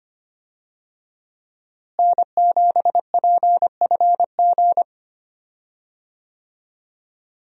Morse code, heard as a keyed tone, about 25 wpm, 700 Hz. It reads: N7PFG
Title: USA Call Signs - 5 Characters - Repeated 2x - 25wpm